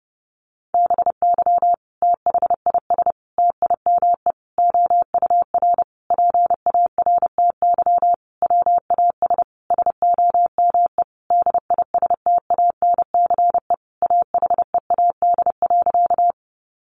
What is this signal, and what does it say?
Morse code, heard as a keyed tone, about 28 wpm, 700 Hz. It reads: BY THIS TIME OUR PARTY WAS SOME DISTANCE AHEAD.
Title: Sentences from Top 1400 Words 30wpm